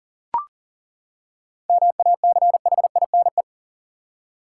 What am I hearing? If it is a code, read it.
Morse code, heard as a keyed tone, about 40 wpm, 700 Hz. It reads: MACHINE
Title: Seven Letter Words - 40wpm